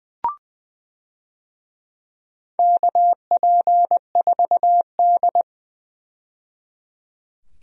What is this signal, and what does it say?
Morse code, heard as a keyed tone, about 20 wpm, 700 Hz. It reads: KP4D
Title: DXCC-100 Call Signs - 4 Characters - 20wpm